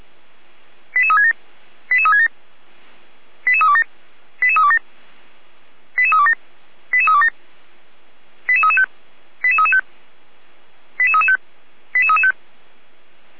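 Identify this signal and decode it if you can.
Five-tone selective call (ZVEI), heard as quick runs of five tones, tones of about 70 ms. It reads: 88367 88367 88327 88327 88327 88327 88335 88335 88335 88335
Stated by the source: ZVEI five-tone selective call 88367 received on SDR, 455 channel.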